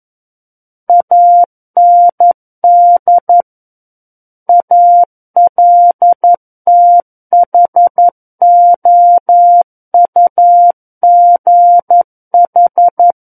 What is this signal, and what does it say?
Morse code, heard as a keyed tone, about 11 wpm, 700 Hz. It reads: AND ALTHOUGH
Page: War of the Worlds - 02-Chapter 2 - 11 WPM